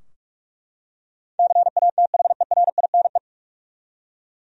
Morse code, heard as a keyed tone, about 43 wpm, 700 Hz. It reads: CATHERINE
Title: USA Common Names - 45wpm